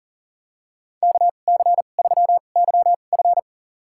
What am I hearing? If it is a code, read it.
Morse code, heard as a keyed tone, about 40 wpm, spaced as for 29 wpm, 700 Hz. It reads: KC3YF
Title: USA Call Signs - 5 Characters - 2x Character Spacing - 40wpm